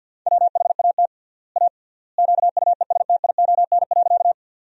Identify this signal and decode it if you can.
Morse code, heard as a keyed tone, about 47 wpm, 700 Hz. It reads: WHAT A QUESTION.
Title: 3x Word-Spacing - Sentences from Top 400 Words - 50wpm